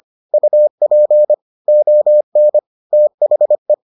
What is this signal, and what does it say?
Morse code, heard as a keyed tone, about 25 wpm, 600 Hz. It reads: UP ON THE